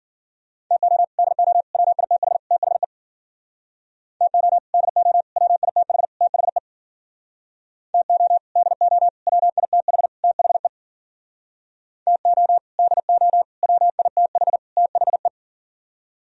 Morse code, heard as a keyed tone, about 52 wpm, 700 Hz. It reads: TO DO WITH THE TO DO WITH THE TO DO WITH THE TO DO WITH THE